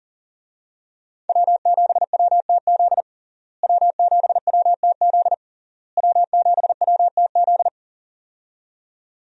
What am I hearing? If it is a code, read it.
Morse code, heard as a keyed tone, about 40 wpm, 700 Hz. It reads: W7WTZ W7WTZ W7WTZ
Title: USA Call Signs - 5 Characters - Repeated 3x - 40wpm